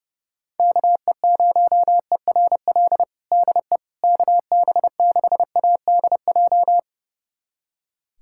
Morse code, heard as a keyed tone, about 32 wpm, 700 Hz. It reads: KE0ERL DE KB6ADJ
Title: USA Callsign DE Callsign 30wpm